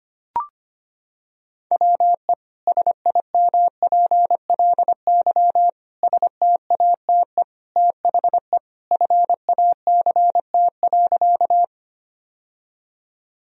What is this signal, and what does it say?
Morse code, heard as a keyed tone, about 25 wpm, 700 Hz. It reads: WE SIMPLY STATE THE FACT.
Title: Sentences from Top 1000 Words - Review Part 4 - 25wpm